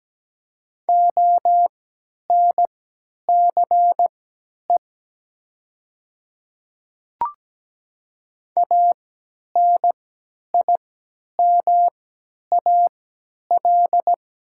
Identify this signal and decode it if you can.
Morse code, heard as a keyed tone, about 17 wpm, spaced as for 10 wpm, 700 Hz. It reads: ONCE ANIMAL
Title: 3x Character Spacing - Top 300 Words - 17wpm